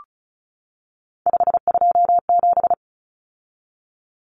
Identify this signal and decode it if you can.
Morse code, heard as a keyed tone, about 34 wpm, 700 Hz. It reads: 527
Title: Numbers - 3-Digit 35wpm